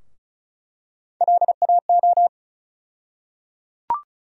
Morse code, heard as a keyed tone, about 36 wpm, 700 Hz. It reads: LAO
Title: USA Call Sign Postfixes - 3 Letters 35wpm